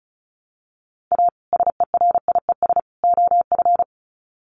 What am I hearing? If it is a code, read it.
Morse code, heard as a keyed tone, about 34 wpm, 700 Hz. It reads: A SERIES OF
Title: Sets of 3 Words 35wpm